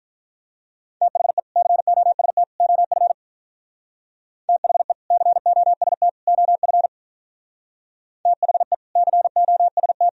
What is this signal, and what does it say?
Morse code, heard as a keyed tone, about 44 wpm, 700 Hz. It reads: THE COST OF THE COST OF THE COST